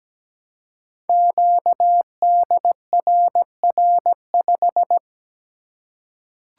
Morse code, heard as a keyed tone, about 17 wpm, 700 Hz. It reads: QDRR5
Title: Code Groups - 5 Letter:Numbers - 17wpm